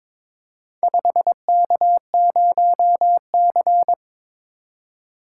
Morse code, heard as a keyed tone, about 22 wpm, 700 Hz. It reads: 5K0C
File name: practice / 22wpm / Four Letter-Number Combinations - 22wpm